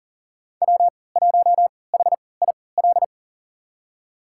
Morse code, heard as a keyed tone, about 40 wpm, spaced as for 23 wpm, 700 Hz. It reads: W1HIL
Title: USA Call Signs - 5 Characters - 3x Character Spacing - 40wpm